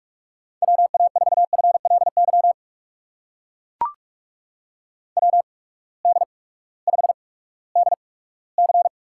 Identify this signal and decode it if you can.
Morse code, heard as a keyed tone, about 42 wpm, spaced as for 20 wpm, 700 Hz. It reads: WA4FLY WD5DC
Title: USA Call Signs - 6 Characters - 8x Character Spacing - 45wpm